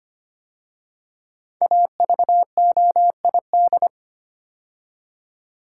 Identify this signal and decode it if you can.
Morse code, heard as a keyed tone, about 25 wpm, 700 Hz. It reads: AVOID
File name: 5 Letter Words - First 3 Letters Given - 25wpm